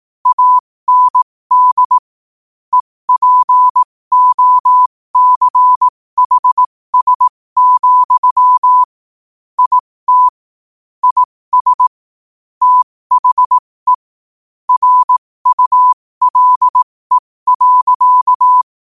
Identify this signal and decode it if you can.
Morse code, heard as a keyed tone, about 18 wpm, spaced as for 15 wpm, 1000 Hz. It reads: AND EPOCHS, IT IS THE RULE.
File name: qotdfiles / 15wpm